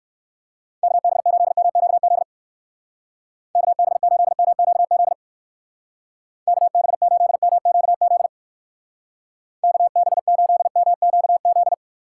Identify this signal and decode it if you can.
Morse code, heard as a keyed tone, about 57 wpm, 700 Hz. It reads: KB8MQZ KB8MQZ KB8MQZ KB8MQZ